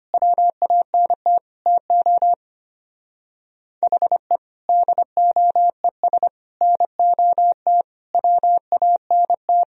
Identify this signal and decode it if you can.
Morse code, heard as a keyed tone, about 24 wpm, 700 Hz. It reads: WANT TO HE DOES NOT WANT
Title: Speed-Racing - Sets of 5 Words 25wpm